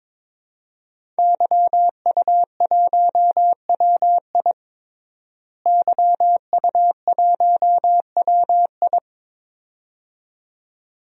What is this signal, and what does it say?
Morse code, heard as a keyed tone, about 22 wpm, 700 Hz. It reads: YU1WI YU1WI
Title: DXCC-100 Call Signs - 5 Characters - Repeated 2x - 22wpm